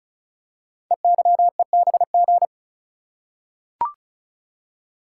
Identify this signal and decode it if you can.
Morse code, heard as a keyed tone, about 36 wpm, 700 Hz. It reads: EYEBG
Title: Code Groups - 5 Letters - 35wpm